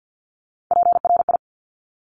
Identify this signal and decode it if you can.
Morse code, heard as a keyed tone, about 50 wpm, 700 Hz. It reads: PRI
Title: Top Three Letter Combinations in Words 50wpm